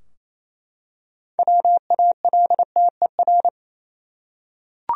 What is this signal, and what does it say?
Morse code, heard as a keyed tone, about 28 wpm, 700 Hz. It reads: WALTER
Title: USA Common Names - 28wpm